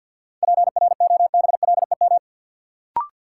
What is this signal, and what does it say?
Morse code, heard as a keyed tone, about 53 wpm, 700 Hz. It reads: PROBLEM
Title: Sentences from Top 1000 Words - New Word Drill - 50wpm